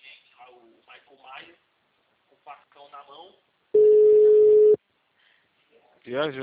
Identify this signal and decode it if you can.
Ringback tone (Europe, 425 Hz), with speech.